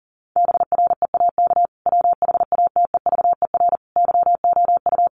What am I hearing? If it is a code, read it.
Morse code, heard as a keyed tone, about 40 wpm, 700 Hz. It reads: BREAK WHATEVER YOU
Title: Sentences from Top 1500 Words 40wpm